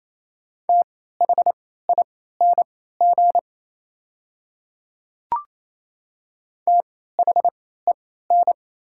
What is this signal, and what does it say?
Morse code, heard as a keyed tone, about 28 wpm, spaced as for 16 wpm, 700 Hz. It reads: THING THEN
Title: 3x Character Spacing - Top 200 Words - 28wpm